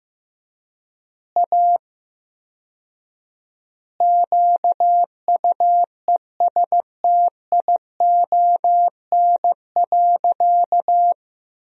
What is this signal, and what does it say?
Morse code, heard as a keyed tone, about 15 wpm, 700 Hz. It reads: A QUESTION.
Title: 4x Word-Spacing - Sentences from Top 400 Words - 15wpm